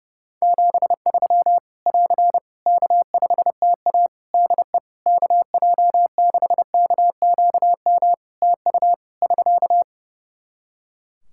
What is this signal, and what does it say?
Morse code, heard as a keyed tone, about 32 wpm, 700 Hz. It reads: 73 <AR> K5TA DE KJ6KQM TU <SK>
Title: Practice QSO - Exchange 4 - Ending - 30wpm